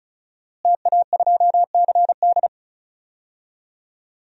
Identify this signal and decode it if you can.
Morse code, heard as a keyed tone, about 34 wpm, 700 Hz. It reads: TA2CD
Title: DXCC-100 Call Signs - 5 Characters - Repeated 2x - 35wpm